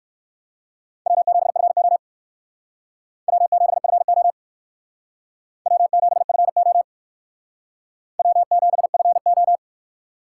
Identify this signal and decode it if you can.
Morse code, heard as a keyed tone, about 55 wpm, 700 Hz. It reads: W7FO W7FO W7FO W7FO
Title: Speed Racing - USA 1x2 Call Signs 45wpm